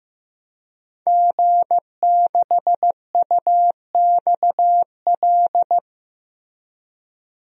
Morse code, heard as a keyed tone, about 15 wpm, 700 Hz. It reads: G6UXL
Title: DXCC-100 Call Signs - 5 Characters - Repeated 3x - 15wpm